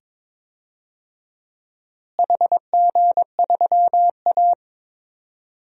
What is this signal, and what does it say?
Morse code, heard as a keyed tone, about 22 wpm, 700 Hz. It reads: HG3A
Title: DXCC-100 Call Signs - 4 Characters - Repeated 2x - 22wpm